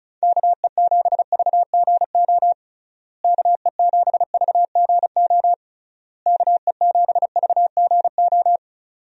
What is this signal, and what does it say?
Morse code, heard as a keyed tone, about 34 wpm, 700 Hz. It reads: KE7VGO KE7VGO KE7VGO
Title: USA Call Signs - 6 Characters - Repeated 3x - 35wpm